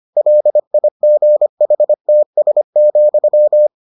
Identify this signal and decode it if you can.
Morse code, heard as a keyed tone, about 25 wpm, 600 Hz. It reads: LIGHTS,